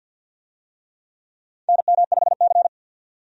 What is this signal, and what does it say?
Morse code, heard as a keyed tone, about 48 wpm, 700 Hz. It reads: NM5C
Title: USA Call Signs - 4 Characters - 50wpm